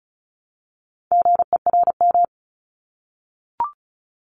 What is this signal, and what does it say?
Morse code, heard as a keyed tone, about 36 wpm, 700 Hz. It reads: GERM
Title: Multi-letter Phonemes Repeated 3x 35wpm